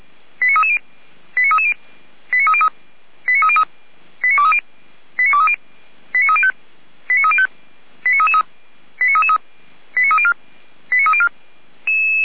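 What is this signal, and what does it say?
Five-tone selective call (ZVEI), heard as quick runs of five tones, tones of about 70 ms. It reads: 89330 89330 89303 89303 89230 89230 89305 89305 89333 89333 89304 89304
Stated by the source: ZVEI five-tone selective call received on SDR, from audio.f85360.feuerwehr-orlamuende.org (Feuerwehr Orlamünde), fire service alerting.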